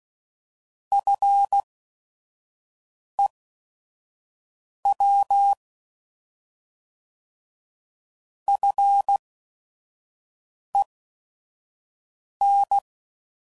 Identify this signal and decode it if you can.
Morse code, heard as a keyed tone, about 16 wpm, spaced as for 5 wpm, 800 Hz. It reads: FEW FEN
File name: Prefixes / By First Letter / Fs